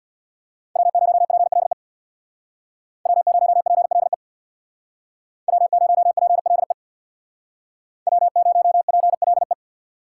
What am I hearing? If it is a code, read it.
Morse code, heard as a keyed tone, about 64 wpm, 700 Hz. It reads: W0PLE W0PLE W0PLE W0PLE